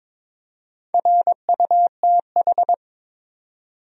Morse code, heard as a keyed tone, about 22 wpm, 700 Hz. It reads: RUTH
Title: USA Common Names - Max 4 Letters - 22wpm